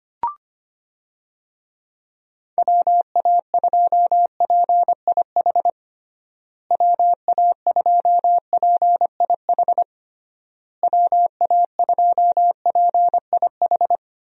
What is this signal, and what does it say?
Morse code, heard as a keyed tone, about 25 wpm, 700 Hz. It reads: WA2PIH WA2PIH WA2PIH
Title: USA Call Signs - 6 Characters - Repeated 3x - 25wpm